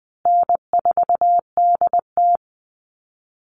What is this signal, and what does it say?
Morse code, heard as a keyed tone, about 20 wpm, 700 Hz. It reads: N4DT
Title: Call Signs Repeated 2x - 20wpm